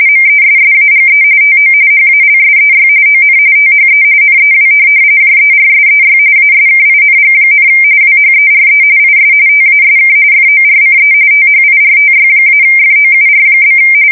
RTTY, about 45.45 baud. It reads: MESSAGE TO HALSEY ON THE NEW JERSEY ASKING, WHERE IS TASK FORCE THIRTY-FOUR?
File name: fsk170 / USS-New-Jersey